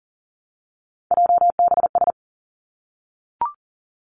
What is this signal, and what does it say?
Morse code, heard as a keyed tone, about 40 wpm, 700 Hz. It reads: JBS
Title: Three Challenging Letter-Number Combinations 40wpm